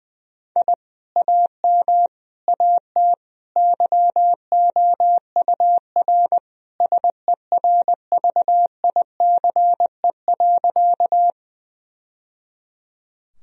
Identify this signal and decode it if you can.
Morse code, heard as a keyed tone, about 21 wpm, 700 Hz. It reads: I AM AT YOUR SERVICE.
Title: Sentences from Top 1000 Words - Review Part 4 - 20wpm